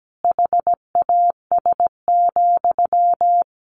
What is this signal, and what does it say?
Morse code, heard as a keyed tone, about 17 wpm, 700 Hz. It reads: HAS,
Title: Sentences from Top 1200 Words 17wpm